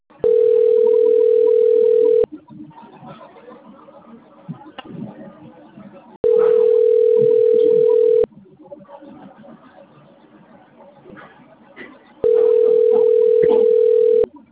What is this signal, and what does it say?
Ringback tone (North America), 440 + 480 Hz, with speech.
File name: out- - - - - .62869